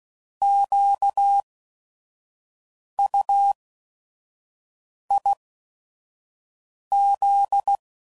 Morse code, heard as a keyed tone, about 16 wpm, spaced as for 5 wpm, 800 Hz. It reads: QUIZ
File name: Prefixes / By First Letter / Qs